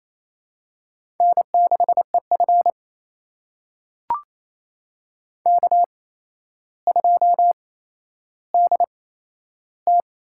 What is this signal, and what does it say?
Morse code, heard as a keyed tone, about 28 wpm, spaced as for 12 wpm, 700 Hz. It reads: N6EF K2DT
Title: USA Call Signs - 8x Character Spacing - 28wpm